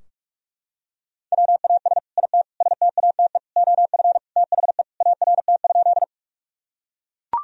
Morse code, heard as a keyed tone, about 41 wpm, 700 Hz. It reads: WAS IT STATE OF THE ART?
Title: Sentences from Top 1000 Words - Review Part 3 - 45wpm